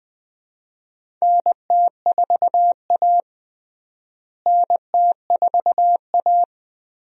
Morse code, heard as a keyed tone, about 20 wpm, 700 Hz. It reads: NT4A NT4A
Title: USA Call Signs - 4 Characters - Repeated 2x - 20wpm